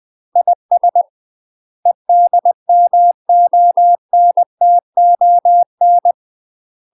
Morse code, heard as a keyed tone, about 20 wpm, 700 Hz. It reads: IS EDMONTON
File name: UK MASTER QSOs TRAINING / 20WPM / UK MASTER QSOs 0052